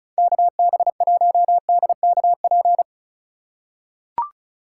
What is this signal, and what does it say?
Morse code, heard as a keyed tone, about 36 wpm, 700 Hz. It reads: KB1DKP
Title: USA Call Signs - 6 Characters - 4x Character Spacing - 35wpm